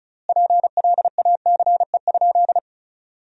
Morse code, heard as a keyed tone, about 34 wpm, 700 Hz. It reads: PLACE?
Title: 3x Word-Spacing - Sentences from Top 200 Words - 35wpm